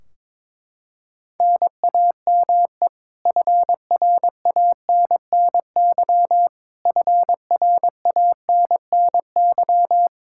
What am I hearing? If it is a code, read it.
Morse code, heard as a keyed tone, about 22 wpm, 700 Hz. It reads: NAME FRANNY FRANNY
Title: Practice QSO - Exchange 1 - Beginning - 22wpm